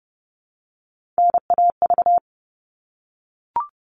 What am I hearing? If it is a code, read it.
Morse code, heard as a keyed tone, about 30 wpm, 700 Hz. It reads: NAV
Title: All Cut Numbers - 3-Digit 30wpm